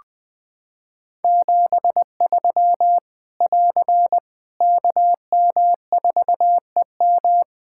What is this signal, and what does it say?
Morse code, heard as a keyed tone, about 20 wpm, 700 Hz. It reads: 73 <AR> KM4EM
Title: Practice QSO - Exchange 4 - Ending - 20wpm